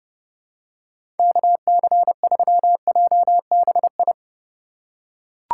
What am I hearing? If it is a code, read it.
Morse code, heard as a keyed tone, about 30 wpm, 700 Hz. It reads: KC3JBI
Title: USA Call Signs - 4x Character Spacing - 30wpm